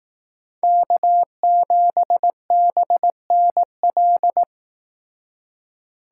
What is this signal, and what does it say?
Morse code, heard as a keyed tone, about 18 wpm, 700 Hz. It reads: K7BNL
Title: Speed Racing - USA 1x3 Call Signs 15wpm